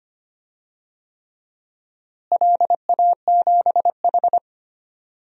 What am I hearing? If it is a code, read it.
Morse code, heard as a keyed tone, about 25 wpm, 700 Hz. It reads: LA7H
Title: DXCC-100 Call Signs - 4 Characters - Repeated 3x - 25wpm